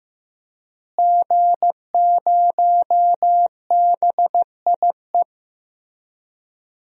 Morse code, heard as a keyed tone, about 15 wpm, 700 Hz. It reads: G0BIE
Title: DXCC-100 Call Signs - 5 Characters - Repeated 3x - 15wpm